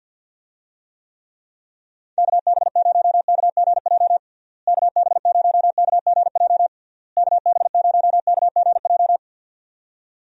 Morse code, heard as a keyed tone, about 47 wpm, 700 Hz. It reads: KB0KGJ KB0KGJ KB0KGJ
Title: USA Call Signs - 6 Characters - Repeated 3x - 50wpm